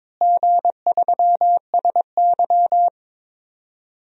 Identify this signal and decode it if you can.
Morse code, heard as a keyed tone, about 22 wpm, 700 Hz. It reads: G3SY
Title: DXCC-100 Call Signs - 4 Characters - Repeated 2x - 22wpm